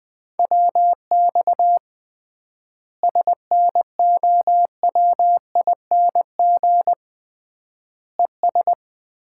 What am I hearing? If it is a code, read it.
Morse code, heard as a keyed tone, about 21 wpm, 700 Hz. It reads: WX SNOWING ES
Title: Common QSO Phrases - 3x Word-Spacing - 20wpm